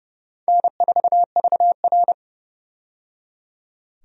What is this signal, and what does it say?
Morse code, heard as a keyed tone, about 32 wpm, 700 Hz. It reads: N4VR
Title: USA Call Signs - 4 Characters - 30wpm